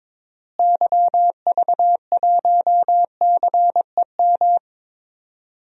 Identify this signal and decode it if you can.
Morse code, heard as a keyed tone, about 21 wpm, 700 Hz. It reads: YV1CEM
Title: DXCC-100 Call Signs - 6 Characters - Repeated 3x - 22wpm